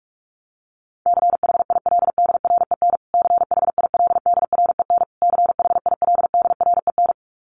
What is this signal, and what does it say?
Morse code, heard as a keyed tone, about 44 wpm, 700 Hz. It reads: CHILDREN CHILDREN CHILDREN
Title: Top 300 Words Repeated 45wpm